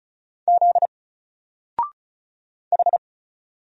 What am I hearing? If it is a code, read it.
Morse code, heard as a keyed tone, about 36 wpm, 700 Hz. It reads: Z H
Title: Single Character (Letter:Number:Punctuation:Prosign) 35wpm